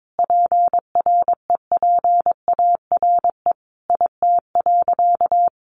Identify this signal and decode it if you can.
Morse code, heard as a keyed tone, about 22 wpm, 700 Hz. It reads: PREPARE IT.
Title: Sentences from Top 1100 Words 22wpm